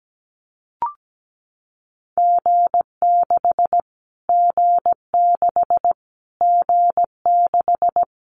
Morse code, heard as a keyed tone, about 17 wpm, 700 Hz. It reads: G6 G6 G6
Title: Callsign Prefixes for Top 25 DXCC - Repeated 3x - 17wpm